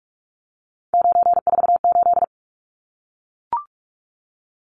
Morse code, heard as a keyed tone, about 44 wpm, 700 Hz. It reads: 948
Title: Numbers - 3-Digit 45wpm